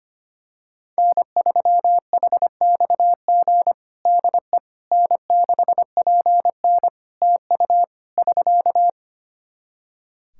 Morse code, heard as a keyed tone, about 25 wpm, 700 Hz. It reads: N3HXG DE N6PN TU <SK>
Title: K1USN SST QSO Practice 25wpm